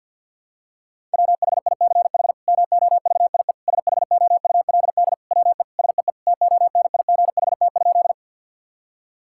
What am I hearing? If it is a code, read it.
Morse code, heard as a keyed tone, about 50 wpm, 700 Hz. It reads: WHICH MOVIE SHOULD WE SEE TONIGHT?
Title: Sentences from Top 1000 Words - Review Part 5 - 50wpm